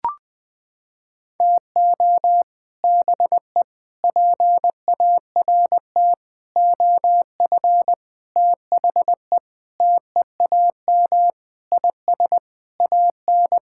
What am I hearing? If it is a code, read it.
Morse code, heard as a keyed tone, about 21 wpm, 700 Hz. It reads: TO BE PART OF THE TEAM IS AN
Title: Sentences with 5-Word Phrases - 20wpm